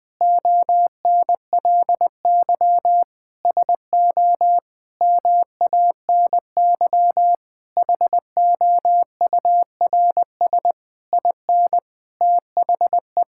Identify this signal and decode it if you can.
Morse code, heard as a keyed tone, about 20 wpm, 700 Hz. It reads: ONLY SO MANY HOURS IN THE
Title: Sentences with 9-Word Phrases - 20wpm